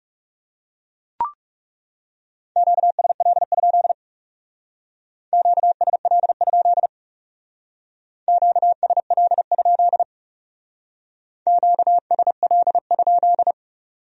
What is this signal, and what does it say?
Morse code, heard as a keyed tone, about 37 wpm, 700 Hz. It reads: QSL? QSL? QSL? QSL?
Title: Speed-Racing Q-Codes - 30wpm